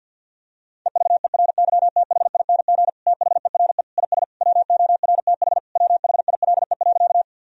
Morse code, heard as a keyed tone, about 51 wpm, 700 Hz. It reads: EVERYTHING THERE IS WORTH WHILE.
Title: Sentences from Top 1000 Words - Review Part 1 - 50wpm